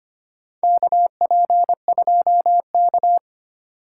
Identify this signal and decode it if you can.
Morse code, heard as a keyed tone, about 25 wpm, 700 Hz. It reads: KP2K
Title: USA Call Signs - 4 Characters - 2x Character Spacing - 25wpm